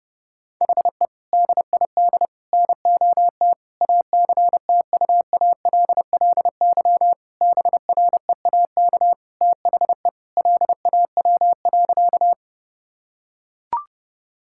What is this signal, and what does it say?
Morse code, heard as a keyed tone, about 32 wpm, 700 Hz. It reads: HE DID NOT ACTUALLY BREAK THE LAW.
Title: Sentences from Top 1000 Words - Review Part 6 - 30wpm